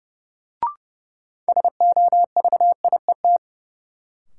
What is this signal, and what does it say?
Morse code, heard as a keyed tone, about 30 wpm, 700 Hz. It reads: SOVIET
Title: Sets of 5 Words Encore - New Word Drill - 30wpm